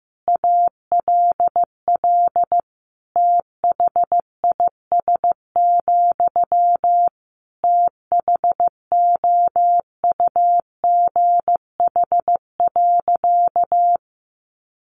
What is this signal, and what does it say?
Morse code, heard as a keyed tone, about 15 wpm, 700 Hz. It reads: ALL THIS, THOUGH.
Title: Sentences from Top 1400 Words 15wpm